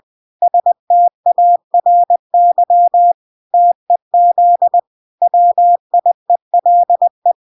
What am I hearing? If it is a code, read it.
Morse code, heard as a keyed tone, about 20 wpm, 700 Hz. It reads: STARY TEZ WIELE